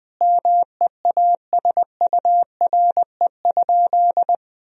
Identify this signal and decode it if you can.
Morse code, heard as a keyed tone, about 20 wpm, 700 Hz. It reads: MEASURE?
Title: Sentences from Top 1000 Words - Review Part 2 - 20wpm